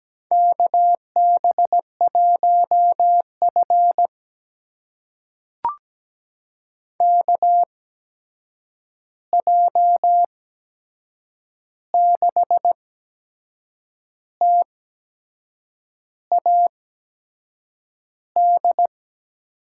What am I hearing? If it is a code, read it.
Morse code, heard as a keyed tone, about 17 wpm, spaced as for 6 wpm, 700 Hz. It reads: KB1F KJ6TAD
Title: USA Call Signs - 8x Character Spacing - 17wpm